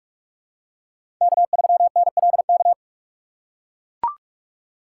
Morse code, heard as a keyed tone, about 42 wpm, 700 Hz. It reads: K3NLK
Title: USA Call Signs - 3x Character Spacing - 45wpm